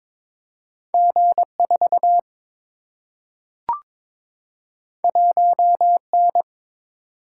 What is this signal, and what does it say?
Morse code, heard as a keyed tone, about 22 wpm, 700 Hz. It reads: G 4 1 N
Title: Single-Letter Single-Number 22wpm